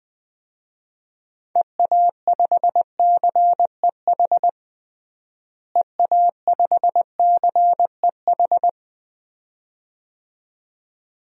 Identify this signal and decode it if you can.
Morse code, heard as a keyed tone, about 20 wpm, 700 Hz. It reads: EA5CEH EA5CEH
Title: DXCC-100 Call Signs - 6 Characters - Repeated 2x - 20wpm